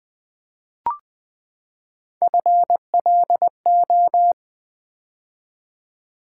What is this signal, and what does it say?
Morse code, heard as a keyed tone, about 20 wpm, 700 Hz. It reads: FLO
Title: Build A Word - 3 Letters At A Time - 8-Letter Words - 20wpm